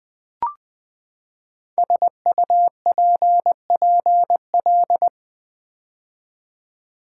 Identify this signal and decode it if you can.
Morse code, heard as a keyed tone, about 20 wpm, 700 Hz. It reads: SUPPL
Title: Build A Word - 5 Letters At A Time - 10-Letter Words - 20wpm